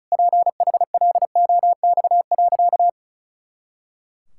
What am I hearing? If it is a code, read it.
Morse code, heard as a keyed tone, about 34 wpm, 700 Hz. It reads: PHLOX.
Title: Pangram Challenge v2 - 35wpm